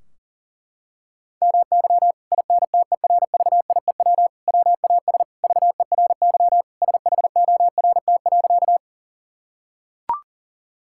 Morse code, heard as a keyed tone, about 40 wpm, 700 Hz. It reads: MY INTERVIEW WAS VERY SHORT.
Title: Sentences from Top 1600 Words 40wpm